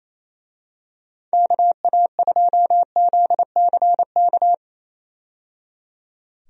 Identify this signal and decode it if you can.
Morse code, heard as a keyed tone, about 27 wpm, 700 Hz. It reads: KA2ZCK
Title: USA Call Signs - 6 Characters - 28wpm